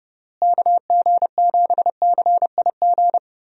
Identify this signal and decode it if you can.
Morse code, heard as a keyed tone, about 32 wpm, 700 Hz. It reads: KG7CIG
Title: USA Call Signs - 6 Characters - Repeated 2x - 30wpm